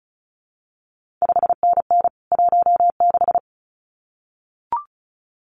Morse code, heard as a keyed tone, about 36 wpm, 700 Hz. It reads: 5NN 16
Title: CQ WW DX Contest Exchange - 35wpm